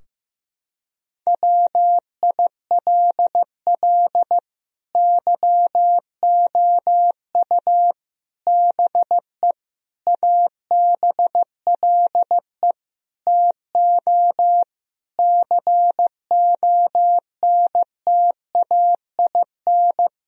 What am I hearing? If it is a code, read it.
Morse code, heard as a keyed tone, about 15 wpm, 700 Hz. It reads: WILL YOU BE ABLE TO CONTAIN
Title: Sentences from Top 1000 Words - Review Part 2 - 15wpm